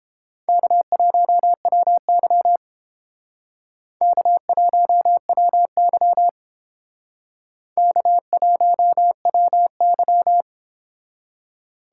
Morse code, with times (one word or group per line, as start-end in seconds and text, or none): 0.5-2.6 K1WY
4.0-6.3 K1WY
7.8-10.4 K1WY